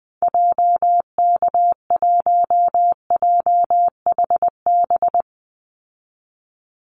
Morse code, heard as a keyed tone, about 20 wpm, 700 Hz. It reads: JK1JHB
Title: Call Signs Repeated 2x - 20wpm